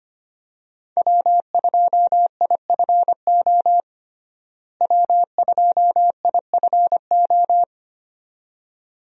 Morse code, heard as a keyed tone, about 25 wpm, 700 Hz. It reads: W2IFO W2IFO
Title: USA Call Signs - 5 Characters - Repeated 2x - 25wpm